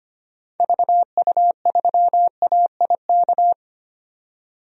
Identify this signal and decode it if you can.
Morse code, heard as a keyed tone, about 25 wpm, 690 Hz. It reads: VU3AIK
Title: DXCC-100 Call Signs - 6 Characters - 25wpm